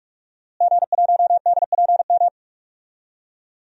Morse code, heard as a keyed tone, about 46 wpm, 700 Hz. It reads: G1DPM
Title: DXCC-100 Call Signs - 5 Characters - Repeated 3x - 45wpm